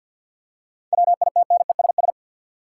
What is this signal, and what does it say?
Morse code, heard as a keyed tone, about 50 wpm, 700 Hz. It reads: WITNESS
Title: Seven Letter Words - 50wpm